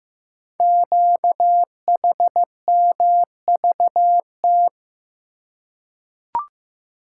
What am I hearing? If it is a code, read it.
Morse code, heard as a keyed tone, about 15 wpm, 700 Hz. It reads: QHMVT
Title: Code Groups - 5 Letters - 15wpm